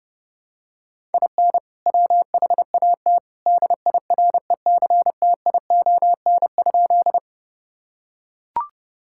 Morse code, heard as a keyed tone, about 30 wpm, 700 Hz. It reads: IN WHAT DIRECTION?
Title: Sentences from Top 1000 Words - Review Part 5 - 30wpm